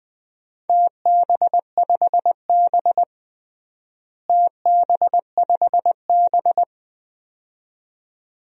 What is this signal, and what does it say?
Morse code, heard as a keyed tone, about 20 wpm, 700 Hz. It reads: TB5B TB5B
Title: DXCC-100 Call Signs - 4 Characters - Repeated 2x - 20wpm